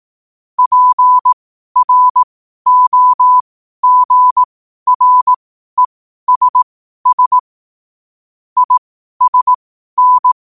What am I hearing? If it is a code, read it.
Morse code, heard as a keyed tone, about 18 wpm, spaced as for 12 wpm, 1000 Hz. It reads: PROGRESS ISN
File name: qotdfiles / 12wpm